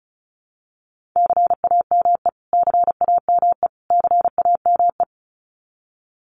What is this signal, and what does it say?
Morse code, heard as a keyed tone, about 34 wpm, 700 Hz. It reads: CAME CAME CAME
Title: Top 500 Words Repeated 35wpm